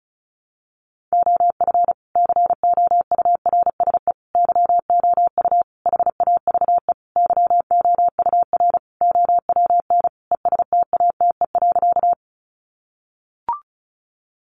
Morse code, heard as a keyed tone, about 34 wpm, 700 Hz. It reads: OF COURSE YOU HAVE YOUR OWN ESTATE.
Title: Sentences from Top 1200 Words 35wpm